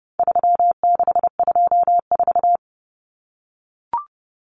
Morse code, heard as a keyed tone, about 30 wpm, 700 Hz. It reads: 3624
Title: Common Cut Numbers - 4-Digit 30wpm